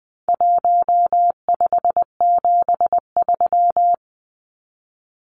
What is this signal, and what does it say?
Morse code, heard as a keyed tone, about 20 wpm, 700 Hz. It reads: 1573
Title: Numbers - Contest Style - 4-Digit 20wpm